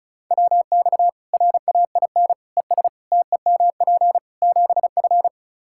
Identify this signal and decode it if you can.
Morse code, heard as a keyed tone, about 34 wpm, 700 Hz. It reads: WX RAIN ES TEMP 7F
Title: Practice QSO - Exchange 2 - Info - 35wpm